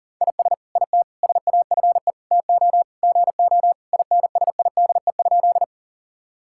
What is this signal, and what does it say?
Morse code, heard as a keyed tone, about 40 wpm, 700 Hz. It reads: IS IT SAFE TO GO INSIDE?
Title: Sentences from Top 1000 Words - Review Part 6 - 40wpm